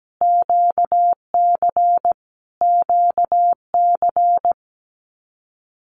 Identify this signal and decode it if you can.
Morse code, heard as a keyed tone, about 17 wpm, 700 Hz. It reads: QC QC
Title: Canadian Provinces and Territories - Repeated 3x - 17wpm